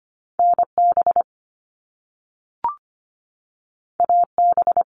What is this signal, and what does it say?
Morse code, heard as a keyed tone, about 25 wpm, 700 Hz. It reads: NB AB
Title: Canadian Provinces and Territories 25wpm